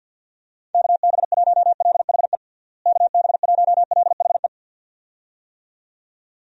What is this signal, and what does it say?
Morse code, heard as a keyed tone, about 50 wpm, 700 Hz. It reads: KB1LHE KB1LHE
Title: USA Call Signs - 6 Characters - Repeated 2x - 50wpm